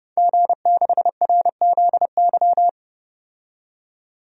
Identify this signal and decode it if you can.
Morse code, heard as a keyed tone, about 32 wpm, 700 Hz. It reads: G6RZY
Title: DXCC-100 Call Signs - 5 Characters - 30wpm